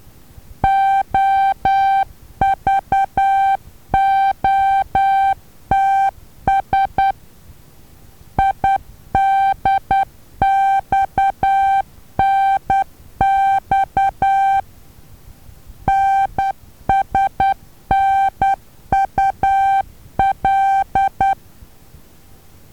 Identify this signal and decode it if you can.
Morse code, heard as a keyed tone, about 10 wpm, 780 Hz. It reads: OVOTS IDXNX NSNUL
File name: Taviro HA6VA B